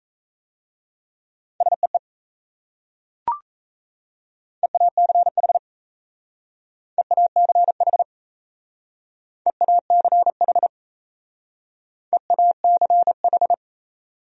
Morse code, heard as a keyed tone, about 36 wpm, 700 Hz. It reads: SEE EACH EACH EACH EACH